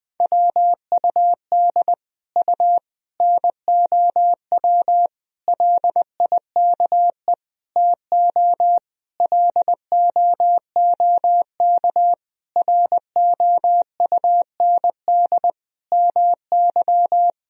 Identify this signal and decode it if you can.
Morse code, heard as a keyed tone, about 21 wpm, 700 Hz. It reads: WUD U NOW LIKE TO LOOK ROUND MY
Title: Sentences from Top 200 Words and Abbreviations 20wpm